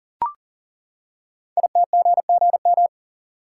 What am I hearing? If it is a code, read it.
Morse code, heard as a keyed tone, about 39 wpm, 700 Hz. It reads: ITGGM
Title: Code Groups - 5 Letters - 40wpm